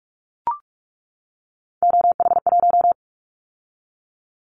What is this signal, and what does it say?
Morse code, heard as a keyed tone, about 44 wpm, 700 Hz. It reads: OH1
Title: Callsign Prefixes for Top 25 DXCC - 45wpm